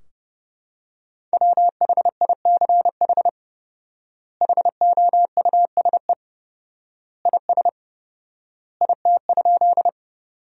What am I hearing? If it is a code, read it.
Morse code, heard as a keyed tone, about 30 wpm, 700 Hz. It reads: WHICH HOUSE IS IT?
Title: 4x Word-Spacing - Sentences from Top 200 Words - 30wpm